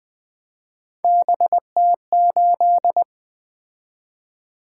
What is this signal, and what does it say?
Morse code, heard as a keyed tone, about 20 wpm, 700 Hz. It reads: BT8
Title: Three Letter-Number Combinations - 20wpm